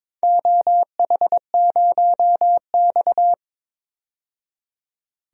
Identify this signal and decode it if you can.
Morse code, heard as a keyed tone, about 22 wpm, 700 Hz. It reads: OH0X
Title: DXCC-100 Call Signs - 4 Characters - 22wpm